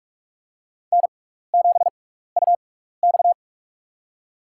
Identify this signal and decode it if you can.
Morse code, heard as a keyed tone, about 44 wpm, spaced as for 15 wpm, 700 Hz. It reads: N7UX
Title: USA Call Signs - 4 Characters - 6x Character Spacing - 45wpm